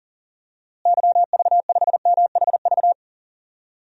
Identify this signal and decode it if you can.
Morse code, heard as a keyed tone, about 37 wpm, 700 Hz. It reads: YV5MHV
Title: DXCC-100 Call Signs - 6 Characters - Repeated 3x - 40wpm